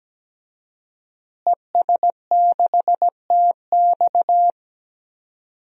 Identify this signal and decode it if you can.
Morse code, heard as a keyed tone, about 17 wpm, 700 Hz. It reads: ES6TX
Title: DXCC-100 Call Signs - 5 Characters - Repeated 2x - 17wpm